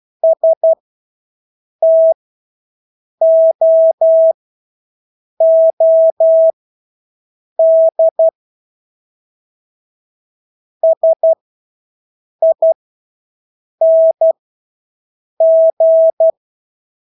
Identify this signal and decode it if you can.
Morse code, heard as a keyed tone, about 12 wpm, spaced as for 6 wpm, 650 Hz. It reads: STOOD SING
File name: Engelske ord 12-6 wpm